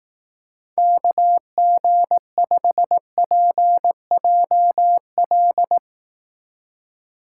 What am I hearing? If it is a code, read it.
Morse code, heard as a keyed tone, about 18 wpm, 700 Hz. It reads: KG5PJL
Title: Speed Racing - USA 2x3 Call Signs 15wpm